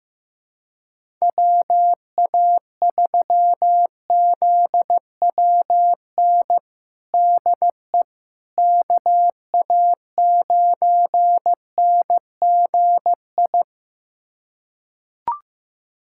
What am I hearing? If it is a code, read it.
Morse code, heard as a keyed tone, about 15 wpm, 700 Hz. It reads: WA3ZWN DE KA9NGI
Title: USA Callsign DE Callsign 15wpm